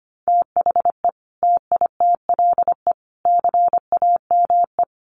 Morse code, heard as a keyed tone, about 25 wpm, 700 Hz. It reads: THE TITLE CAME
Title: Sentences from Top 1200 Words 25wpm